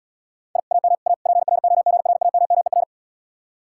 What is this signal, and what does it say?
Morse code, heard as a keyed tone, about 78 wpm, 700 Hz. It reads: I AM A PROGRAMMER